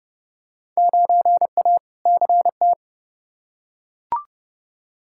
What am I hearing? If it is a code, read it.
Morse code, heard as a keyed tone, about 30 wpm, 700 Hz. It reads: 9A CT
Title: Field Day Exchanges Encore - 30wpm